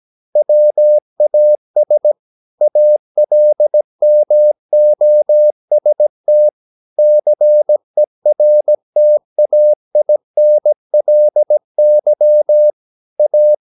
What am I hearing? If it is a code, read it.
Morse code, heard as a keyed tone, about 17 wpm, 600 Hz. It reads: WAS ALMOST CERTAINLY A
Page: Romanen Never af Ken Follet på engelsk.